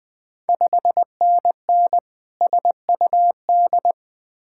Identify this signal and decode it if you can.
Morse code, heard as a keyed tone, about 20 wpm, 700 Hz. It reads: 5NN SUD
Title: Ontario QSO Party Exchange 20wpm